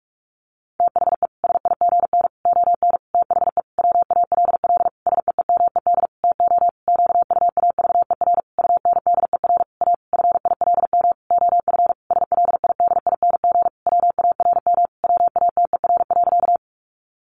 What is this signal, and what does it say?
Morse code, heard as a keyed tone, about 44 wpm, 700 Hz. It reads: THE SIGN ON THE WALL SEEMED TO QUAVER UNDER A FILM OF SLIDING WARM WATER.
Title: A Sound Of Thunder 45wpm